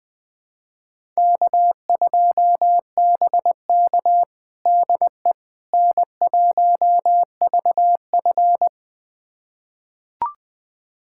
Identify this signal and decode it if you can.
Morse code, heard as a keyed tone, about 21 wpm, 700 Hz. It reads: K2BK DE N1VF
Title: USA Callsign DE Callsign - 4 Characters - 20wpm